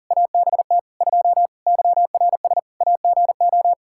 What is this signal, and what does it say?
Morse code, heard as a keyed tone, about 40 wpm, 700 Hz. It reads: ABT 2 YRS AGO
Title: Practice QSO - Exchange 3 - More Info - 40wpm